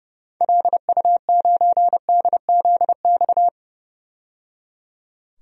Morse code, heard as a keyed tone, about 30 wpm, 700 Hz. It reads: LU9DZX
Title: DXCC-100 Call Signs - 6 Characters - 30wpm